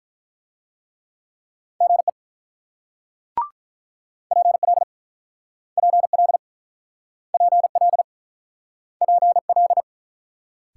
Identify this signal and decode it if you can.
Morse code, heard as a keyed tone, about 48 wpm, 690 Hz. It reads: GE PL PL PL PL